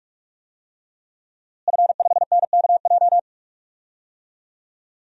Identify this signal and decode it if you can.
Morse code, heard as a keyed tone, about 46 wpm, 700 Hz. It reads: F5NKJ
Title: DXCC-100 Call Signs - 5 Characters - Repeated 3x - 45wpm